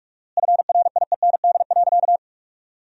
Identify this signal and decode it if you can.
Morse code, heard as a keyed tone, about 42 wpm, 700 Hz. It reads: FRIEND.
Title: 4x Word-Spacing - Sentences from Top 300 Words - 45wpm